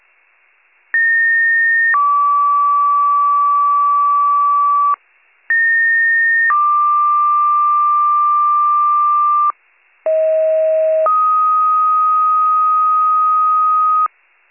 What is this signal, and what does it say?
Two-tone page, heard as a one-second tone, then a three-second tone, about 1850 then 1150 Hz, about 1850 then 1180 Hz, about 640 then 1250 Hz. It is heard from 0.9 to 14.1 s.